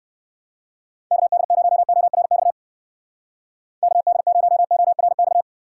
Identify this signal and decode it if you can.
Morse code, heard as a keyed tone, about 56 wpm, 700 Hz. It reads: KD0ORX KD0ORX
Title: Speed Racing - USA Call Signs 45wpm